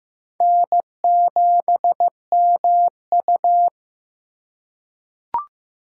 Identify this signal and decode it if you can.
Morse code, heard as a keyed tone, about 15 wpm, 700 Hz. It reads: N7MU
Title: USA Call Signs - 3x Character Spacing - 15wpm